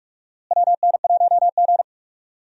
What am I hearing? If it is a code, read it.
Morse code, heard as a keyed tone, about 44 wpm, 700 Hz. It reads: WN1G
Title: USA Call Signs - 4 Characters - Repeated 3x - 45wpm